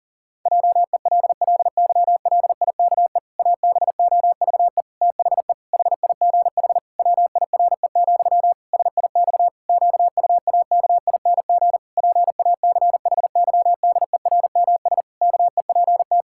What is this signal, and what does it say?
Morse code, heard as a keyed tone, about 40 wpm, 700 Hz. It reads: JELLYLIKE ABOVE THE HIGH WIRE, SIX QUAKING PACHYDERMS KEPT
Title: Pangram Challenge v2 - 40wpm